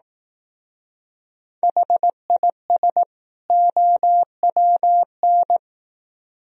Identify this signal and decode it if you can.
Morse code, heard as a keyed tone, about 18 wpm, 700 Hz. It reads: HIS OWN
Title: Speed-Racing - Sets of 2 Words 15wpm